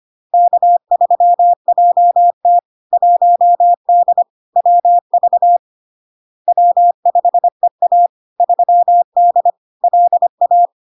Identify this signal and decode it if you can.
Morse code, heard as a keyed tone, about 25 wpm, 700 Hz. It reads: K3JT 1D WV W5EA 3D LA
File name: LARK-FD-2022-Log-in-CW-25wpm-700-Hz